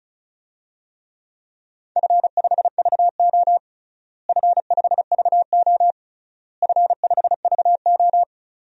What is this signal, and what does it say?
Morse code, heard as a keyed tone, about 36 wpm, 700 Hz. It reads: F5VO F5VO F5VO
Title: DXCC-100 Call Signs - 4 Characters - Repeated 3x - 35wpm